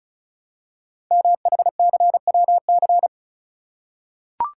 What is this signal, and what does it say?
Morse code, heard as a keyed tone, about 36 wpm, 700 Hz. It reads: MHCWC
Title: Code Groups - 5 Letter:Numbers - 35wpm